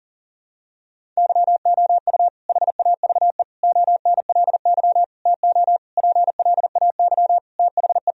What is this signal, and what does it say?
Morse code, heard as a keyed tone, about 40 wpm, 700 Hz. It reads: YOU HAVE ONLY TO PLAY THE
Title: Sentences from Top 1000 Words - Review Part 1 - 40wpm